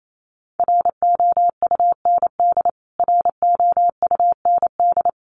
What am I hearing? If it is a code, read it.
Morse code, heard as a keyed tone, about 28 wpm, 700 Hz. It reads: ROUND ROUND
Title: Top 200 Words Repeated 28wpm